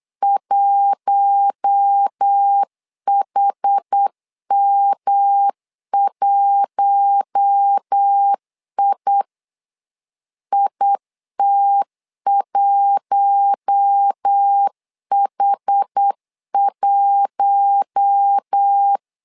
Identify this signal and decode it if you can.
Morse code, heard as a keyed tone, about 8 wpm, 800 Hz. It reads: 1HM1I IT1H1
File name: LEZIONI / 4 Q1 8wpm 3035